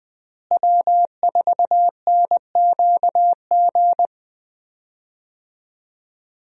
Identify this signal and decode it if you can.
Morse code, heard as a keyed tone, about 20 wpm, 700 Hz. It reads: W4NQG
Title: USA Call Signs - 5 Characters - Repeated 2x - 20wpm